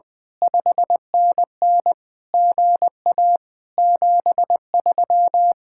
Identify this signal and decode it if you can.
Morse code, heard as a keyed tone, about 20 wpm, 700 Hz. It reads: 5NN GA 73
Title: Speed-Racing Common QSO Phrases - 20wpm